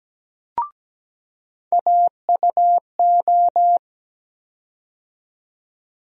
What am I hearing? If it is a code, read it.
Morse code, heard as a keyed tone, about 17 wpm, 700 Hz. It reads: AUO
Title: USA Call Sign Postfixes - 3 Letters 17wpm